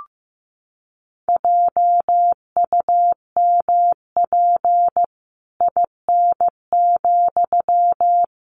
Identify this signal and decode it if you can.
Morse code, heard as a keyed tone, about 15 wpm, 700 Hz. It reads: JUMP IN,
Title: Sentences from Top 1300 Words 15wpm